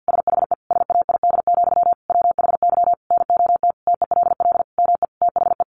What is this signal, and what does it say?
Morse code, heard as a keyed tone, about 50 wpm, 700 Hz. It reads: SHE SAID, WHY NOT TELL ME THE
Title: Sentences from Top 800 Words 50wpm